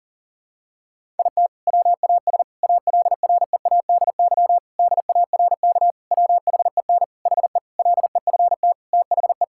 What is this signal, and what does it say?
Morse code, heard as a keyed tone, about 40 wpm, 700 Hz. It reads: IT WAS ALREADY DARK WHEN HE LEFT THE
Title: Sentences from Top 1000 Words - Review Part 5 - 40wpm